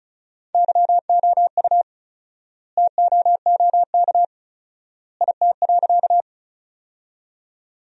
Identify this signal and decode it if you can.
Morse code, heard as a keyed tone, about 36 wpm, 700 Hz. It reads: YOU TOOK IT.
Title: 4x Word-Spacing - Sentences from Top 300 Words - 35wpm